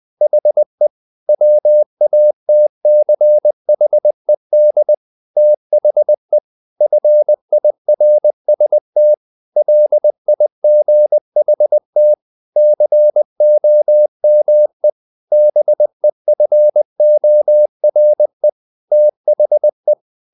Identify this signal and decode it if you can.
Morse code, heard as a keyed tone, about 20 wpm, 600 Hz. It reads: HE WATCHED THE FIRST LIGHT COME BEFORE THE
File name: the old man and the sea part 2 20wpm